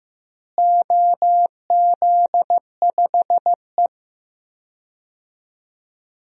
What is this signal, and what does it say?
Morse code, heard as a keyed tone, about 15 wpm, 700 Hz. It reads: OZ5E
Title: DXCC-100 Call Signs - 4 Characters - Repeated 3x - 15wpm